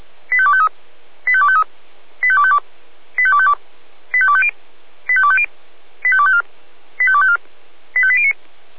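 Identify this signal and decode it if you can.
Five-tone selective call (ZVEI), heard as quick runs of five tones, tones of about 70 ms. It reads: 85253 85253 85252 85252 85250 85250 85254 85254 85809
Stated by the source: ZVEI five-tone call 85810 received by SDR, from the feuerwehr-orlamuende.org (fire brigade Orlamünde) audio archive.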